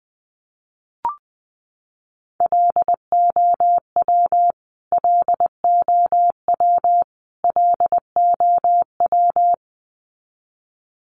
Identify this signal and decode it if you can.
Morse code, heard as a keyed tone, about 20 wpm, 700 Hz. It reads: LOW LOW LOW
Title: Top 300 Words Repeated 20wpm